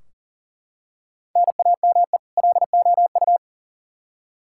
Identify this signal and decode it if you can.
Morse code, heard as a keyed tone, about 40 wpm, 700 Hz. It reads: NAME LOU
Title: Practice QSO - Exchange 1 - Beginning - 40wpm